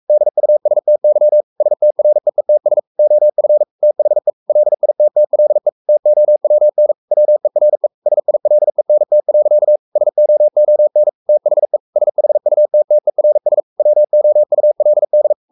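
Morse code, heard as a keyed tone, about 44 wpm, 600 Hz. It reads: DUSTY STREETS OF THE LITTLE TOWN WERE SILENT. SOON THE SHUTTERS WOULD
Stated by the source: Morse code at 43 wpm sending Ken Follett's novel Never in English, from CW med Gnister.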